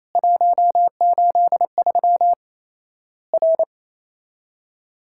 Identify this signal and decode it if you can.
Morse code, heard as a keyed tone, about 28 wpm, 700 Hz. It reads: 183 R
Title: Worked All Europe DX Contest - QTC Exchange - 28wpm